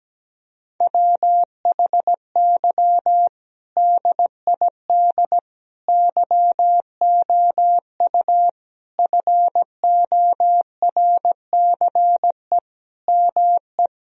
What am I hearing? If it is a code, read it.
Morse code, heard as a keyed tone, about 17 wpm, 700 Hz. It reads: WHY DID YOU FORCE ME
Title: Sentences from Top 1000 Words - Review Part 3 - 17wpm